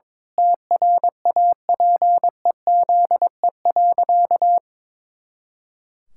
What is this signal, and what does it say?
Morse code, heard as a keyed tone, about 22 wpm, 700 Hz. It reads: TRAPEZE.
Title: Pangram Challenge v2 - 22wpm